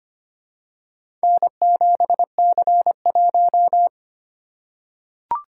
Morse code, heard as a keyed tone, about 25 wpm, 700 Hz. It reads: N7C1
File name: practice / 25wpm / Four Letter-Number Combinations - 25wpm